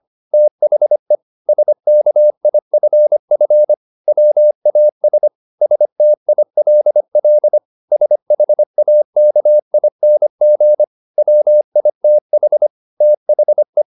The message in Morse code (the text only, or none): THE SKIFF WAS STILL SHAKING WITH THE